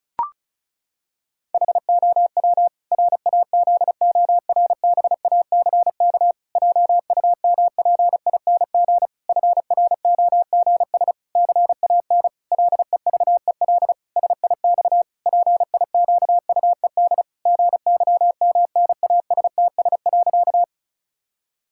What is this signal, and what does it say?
Morse code, heard as a keyed tone, about 34 wpm, 700 Hz. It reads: HOW RAZORBACK JUMPING FROGS CAN LEVEL SIX PIQUED GYMNASTS.
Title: Pangram Challenge v2 - 35wpm